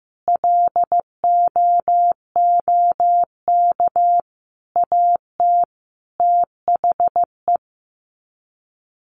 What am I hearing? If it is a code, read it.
Morse code, heard as a keyed tone, about 15 wpm, 700 Hz. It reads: LOOK AT THE
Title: Sets of 5 Words 15wpm